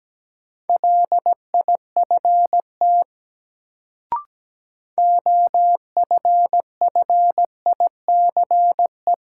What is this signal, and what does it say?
Morse code, heard as a keyed tone, about 17 wpm, 700 Hz. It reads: LIFT OFFICE
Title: Sentences from Top 1000 Words - New Word Drill - 17wpm